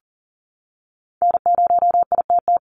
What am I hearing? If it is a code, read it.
Morse code, heard as a keyed tone, about 40 wpm, 700 Hz. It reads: N0ITT
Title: USA Call Signs 40wpm